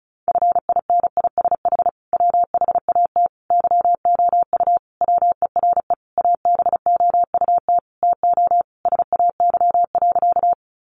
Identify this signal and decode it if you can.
Morse code, heard as a keyed tone, about 34 wpm, 700 Hz. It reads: FINISH WHAT YOU WERE ABOUT TO SAY.
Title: Sentences from Top 1200 Words 35wpm